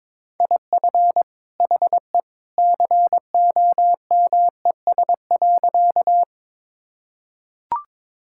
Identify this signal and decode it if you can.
Morse code, heard as a keyed tone, about 22 wpm, 700 Hz. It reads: IF HE COMES.
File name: practice / 22wpm / Sentences with 2-Word Phrases - 22wpm